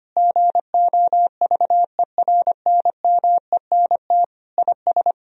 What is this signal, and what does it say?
Morse code, heard as a keyed tone, about 25 wpm, 700 Hz. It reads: GOVERNMENT IS
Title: Sentences from Top 1000 Words - Review Part 3 - 25wpm